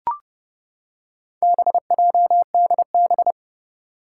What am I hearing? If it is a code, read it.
Morse code, heard as a keyed tone, about 30 wpm, 700 Hz. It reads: BJDB
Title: Four Challenging Letter-Number Combinations 30wpm